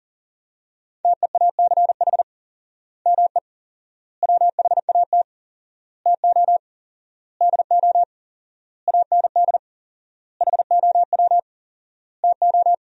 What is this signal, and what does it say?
Morse code, heard as a keyed tone, about 40 wpm, 700 Hz. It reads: TEACH ME WHAT TO DO AND HOW TO
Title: 4x Word-Spacing - Sentences from Top 500 Words - 40wpm